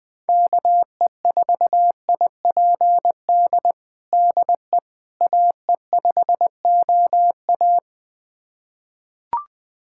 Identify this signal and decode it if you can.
Morse code, heard as a keyed tone, about 21 wpm, 700 Hz. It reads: KE4IPD DE AE5OA
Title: USA Callsign DE Callsign 20wpm